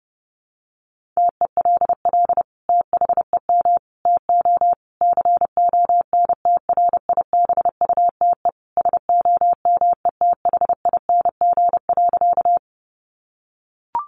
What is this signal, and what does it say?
Morse code, heard as a keyed tone, about 30 wpm, 700 Hz. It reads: TELL THEM TO CONTRIBUTE SOMETHING.
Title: Sentences from Top 1400 Words 30wpm